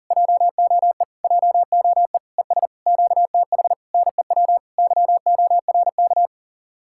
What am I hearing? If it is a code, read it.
Morse code, heard as a keyed tone, about 40 wpm, 700 Hz. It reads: JOE JOE ES QTH NEW YORK